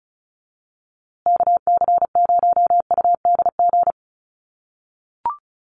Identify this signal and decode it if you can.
Morse code, heard as a keyed tone, about 34 wpm, 700 Hz. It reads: KC0UDG
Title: USA Call Signs Repeated 3x 35wpm